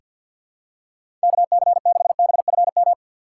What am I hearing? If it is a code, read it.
Morse code, heard as a keyed tone, about 47 wpm, 700 Hz. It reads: KC6BFM
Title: USA Call Signs - 6 Characters - 4x Character Spacing - 50wpm